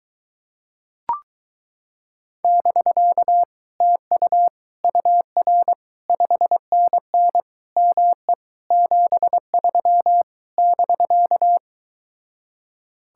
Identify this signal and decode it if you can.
Morse code, heard as a keyed tone, about 23 wpm, 700 Hz. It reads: <BK> TU UR 5NN ME 73 <BK>